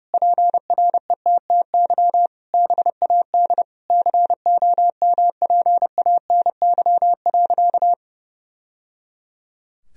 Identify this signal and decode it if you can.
Morse code, heard as a keyed tone, about 30 wpm, 700 Hz. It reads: PRETTY BAD COMPANY.
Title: Sentences from Top 1000 Words - Review Part 3 - 30wpm